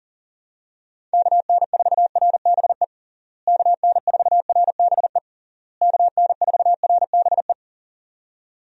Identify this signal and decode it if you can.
Morse code, heard as a keyed tone, about 40 wpm, 700 Hz. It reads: KN4RBE KN4RBE KN4RBE
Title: USA Call Signs - 6 Characters - Repeated 3x - 40wpm